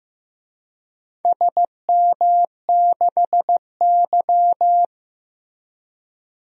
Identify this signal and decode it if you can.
Morse code, heard as a keyed tone, about 15 wpm, 700 Hz. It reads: SM6Y
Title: DXCC-100 Call Signs - 4 Characters - Repeated 3x - 15wpm